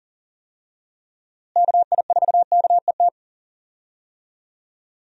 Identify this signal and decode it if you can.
Morse code, heard as a keyed tone, about 37 wpm, 700 Hz. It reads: KI4KET
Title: USA Call Signs - 6 Characters - Repeated 3x - 40wpm